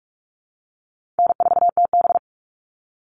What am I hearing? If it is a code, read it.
Morse code, heard as a keyed tone, about 44 wpm, 700 Hz. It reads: N4TB
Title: Call Signs - 45wpm